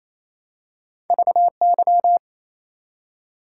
Morse code, heard as a keyed tone, about 28 wpm, 700 Hz. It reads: VY
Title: Top 90 QSO Elements 28wpm